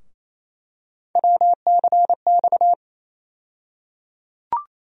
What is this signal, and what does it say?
Morse code, heard as a keyed tone, about 28 wpm, 700 Hz. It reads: WCX
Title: USA Call Sign Postfixes - 3 Letters 28wpm